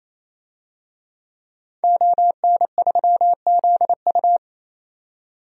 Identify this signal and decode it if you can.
Morse code, heard as a keyed tone, about 28 wpm, 700 Hz. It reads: ON3ZU
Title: DXCC-100 Call Signs - 5 Characters - 28wpm